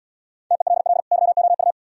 Morse code, heard as a keyed tone, about 74 wpm, 700 Hz. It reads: TELL YOU